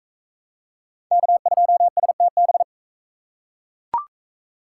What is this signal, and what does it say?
Morse code, heard as a keyed tone, about 40 wpm, 700 Hz. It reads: K2STB